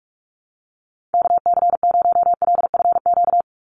Morse code, heard as a keyed tone, about 44 wpm, 700 Hz. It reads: KC0LFQ
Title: USA Call Signs 45wpm